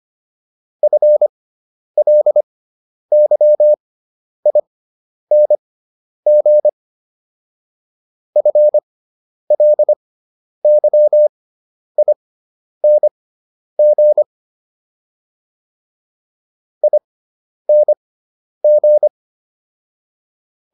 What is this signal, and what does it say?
Morse code, heard as a keyed tone, about 25 wpm, spaced as for 10 wpm, 600 Hz. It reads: FLYING FLYING ING